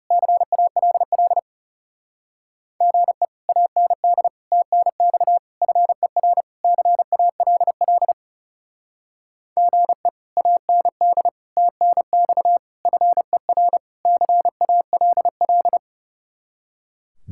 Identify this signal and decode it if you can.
Morse code, heard as a keyed tone, about 38 wpm, 700 Hz. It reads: CALL GE AND TNX FER CALL GE AND TNX FER CALL